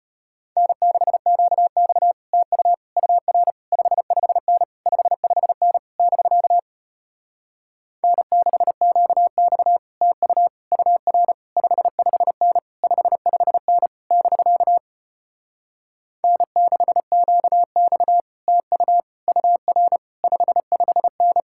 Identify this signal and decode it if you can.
Morse code, heard as a keyed tone, about 33 wpm, 700 Hz. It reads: N6QX TU UR 55N 55N <BK> N6QX TU UR 55N 55N <BK> N6QX TU UR 55N